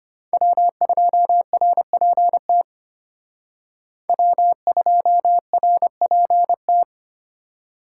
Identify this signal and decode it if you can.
Morse code, heard as a keyed tone, about 31 wpm, 700 Hz. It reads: W2RPT W2RPT